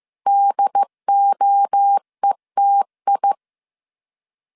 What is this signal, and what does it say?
Morse code, heard as a keyed tone, about 15 wpm, 800 Hz. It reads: DOETI
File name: CW / DET / 1 3 1500C S A 14wpm 2447